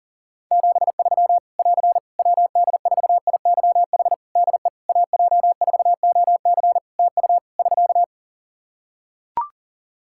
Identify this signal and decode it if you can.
Morse code, heard as a keyed tone, about 37 wpm, 700 Hz. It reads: 73 <AR> WD4IYH DE AJ4OC TU <SK>
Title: Practice QSO - Exchange 4 - Ending - 40wpm